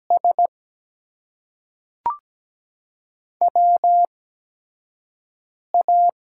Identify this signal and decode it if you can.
Morse code, heard as a keyed tone, about 17 wpm, spaced as for 5 wpm, 700 Hz. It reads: S WA
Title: USA Call Signs - 8x Character Spacing - 17wpm